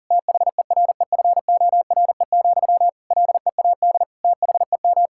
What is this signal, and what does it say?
Morse code, heard as a keyed tone, about 37 wpm, 700 Hz. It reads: THEREFORE, LEAD THEM
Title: Sentences from Top 1000 Words - Review Part 5 - 40wpm